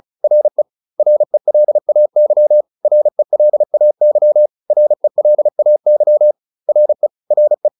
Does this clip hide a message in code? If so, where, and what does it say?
Yes: Morse code from 0.2 to 7.7 s, heard as a keyed tone, about 34 wpm, 600 Hz. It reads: RE RELAY RELAY RELAY RE RE